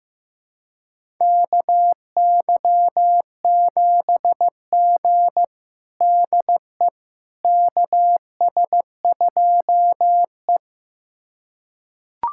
Morse code, heard as a keyed tone, about 15 wpm, 700 Hz. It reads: KY7G DE KS2E
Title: USA Callsign DE Callsign - 4 Characters - 15wpm